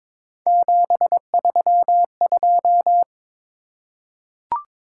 Morse code, heard as a keyed tone, about 22 wpm, 700 Hz. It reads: 732
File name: practice / 22wpm / Three Letter-Number Combinations - 22wpm